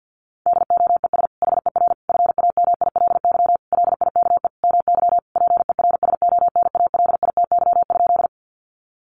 Morse code, heard as a keyed tone, about 50 wpm, 700 Hz. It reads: DOES HER FAMILY LIKE MY PERSONALITY?
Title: Sentences from Top 1400 Words 50wpm